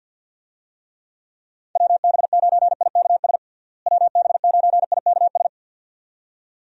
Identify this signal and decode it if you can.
Morse code, heard as a keyed tone, about 53 wpm, 700 Hz. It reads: WB9IKS WB9IKS
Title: USA Call Signs - 6 Characters - Repeated 2x - 50wpm